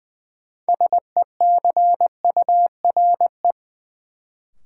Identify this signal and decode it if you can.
Morse code, heard as a keyed tone, about 20 wpm, 700 Hz. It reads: SECURE
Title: Sentences from Top 1200 Words - New Word Drill Alt - 20wpm